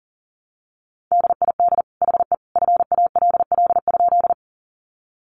Morse code, heard as a keyed tone, about 40 wpm, 700 Hz. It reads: DID HE FALL?
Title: Sentences from Top 500 Words 40wpm